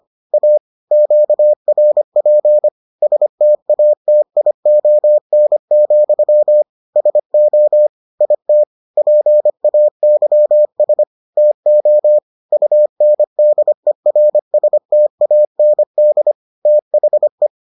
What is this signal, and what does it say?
Morse code, heard as a keyed tone, about 25 wpm, 600 Hz. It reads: A QRP STATION, SO IT PAYS TO UNDERSTAND THE